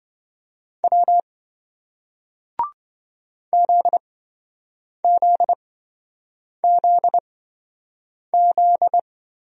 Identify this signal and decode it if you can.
Morse code, heard as a keyed tone, about 27 wpm, spaced as for 7 wpm, 700 Hz. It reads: W ZZZZ